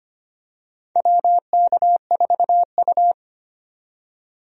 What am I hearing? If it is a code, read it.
Morse code, heard as a keyed tone, about 25 wpm, 700 Hz. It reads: WK4U
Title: USA Call Signs - 4 Characters - 25wpm